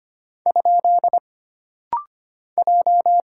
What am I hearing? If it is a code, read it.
Morse code, heard as a keyed tone, about 25 wpm, 700 Hz. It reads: ? J
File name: Single Character (Letter:Number:Punctuation:Prosign) - Rapid-Fire - 25wpm